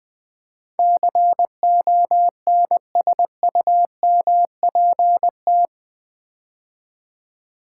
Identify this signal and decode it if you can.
Morse code, heard as a keyed tone, about 20 wpm, 700 Hz. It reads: CONSUMPT
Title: Build A Word - 4 Letters At A Time - 11-Letter Words - 20wpm